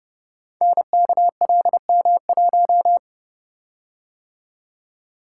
Morse code, heard as a keyed tone, about 30 wpm, 700 Hz. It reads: NKLM1
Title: Code Groups - 5 Letter:Numbers - 30wpm